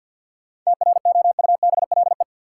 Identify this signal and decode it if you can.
Morse code, heard as a keyed tone, about 52 wpm, 700 Hz. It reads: TROUBLE
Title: Sentences from Top 1200 Words - New Word Drill - 50wpm